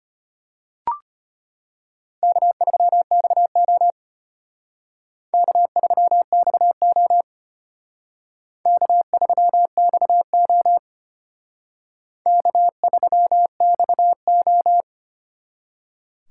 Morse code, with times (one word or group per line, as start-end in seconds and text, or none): 2.2-3.9 K3XO
5.3-7.2 K3XO
8.7-10.8 K3XO
12.3-14.8 K3XO